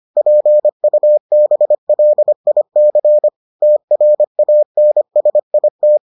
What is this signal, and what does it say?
Morse code, heard as a keyed tone, about 25 wpm, 600 Hz. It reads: PUBLIC TRANSIT